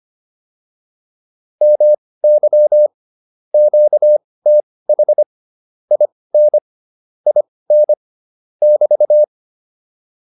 Morse code, heard as a keyed tone, about 25 wpm, spaced as for 18 wpm, 600 Hz. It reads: MY QTH IN IN <BT>